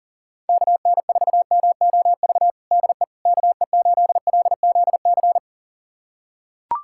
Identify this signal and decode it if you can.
Morse code, heard as a keyed tone, about 37 wpm, 700 Hz. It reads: KN4MOV DE KE8LZC
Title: USA Callsign DE Callsign - 6 Characters - 40wpm